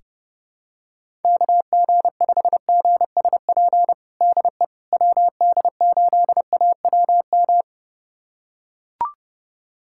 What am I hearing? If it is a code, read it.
Morse code, heard as a keyed tone, about 30 wpm, 700 Hz. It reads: KG5GSP DE WD8AWM
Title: USA Callsign DE Callsign 30wpm